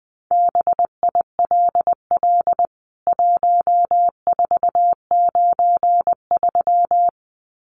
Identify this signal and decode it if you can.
Morse code, heard as a keyed tone, about 20 wpm, 700 Hz. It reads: BILL 1493
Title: CWT Exchange 20wpm